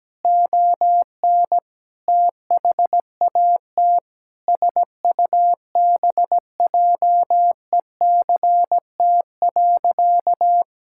Morse code, heard as a keyed tone, about 17 wpm, 700 Hz. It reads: ON THAT SUBJECT.
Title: Sentences from Top 1600 Words 17wpm